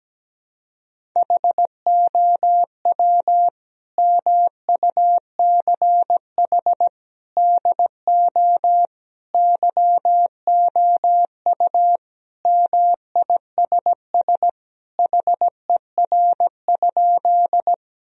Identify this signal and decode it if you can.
Morse code, heard as a keyed tone, about 17 wpm, 700 Hz. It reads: HOW MUCH DO YOU MISS HER?
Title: Sentences from Top 1000 Words - Review Part 3 - 17wpm